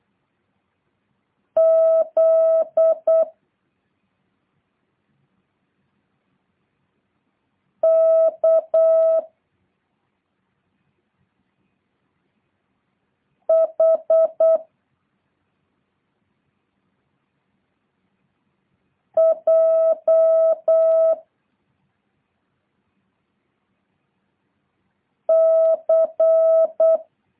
Morse code, heard as a keyed tone, about 8 wpm, 650 Hz. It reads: Z K H J C